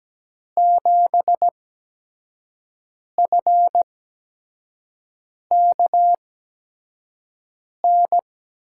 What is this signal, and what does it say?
Morse code, heard as a keyed tone, about 17 wpm, spaced as for 5 wpm, 700 Hz. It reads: 7FKN
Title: USA Call Signs - 5 Characters - 8x Character Spacing - 17wpm